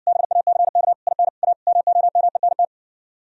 Morse code, heard as a keyed tone, about 60 wpm, 700 Hz. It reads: BACK IN A MOMENT